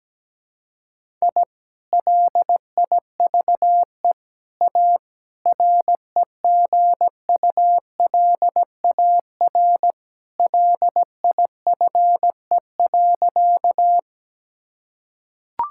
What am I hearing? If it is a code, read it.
Morse code, heard as a keyed tone, about 17 wpm, 700 Hz. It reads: I LIVE A REGULAR LIFE.
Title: Sentences from Top 1000 Words - Review Part 6 - 17wpm